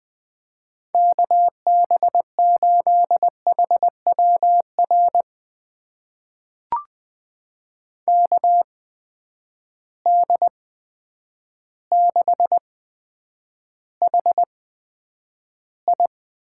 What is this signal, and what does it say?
Morse code, heard as a keyed tone, about 20 wpm, spaced as for 9 wpm, 700 Hz. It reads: KB8HWR KD6HI
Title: USA Call Signs - 6 Characters - 8x Character Spacing - 20wpm